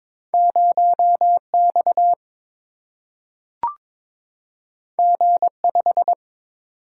Morse code, heard as a keyed tone, about 22 wpm, 700 Hz. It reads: 0 X G 5
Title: Single-Letter Single-Number 22wpm